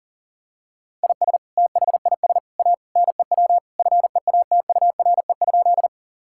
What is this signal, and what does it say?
Morse code, heard as a keyed tone, about 37 wpm, 700 Hz. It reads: IS THIS A NEW FEATURE?
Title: Sentences from Top 1000 Words - Review Part 5 - 40wpm